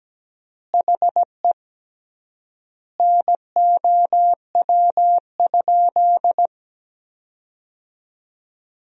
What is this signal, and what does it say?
Morse code, heard as a keyed tone, about 17 wpm, 700 Hz. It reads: HE NOW?
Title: 3x Word-Spacing - Sentences from Top 200 Words - 17wpm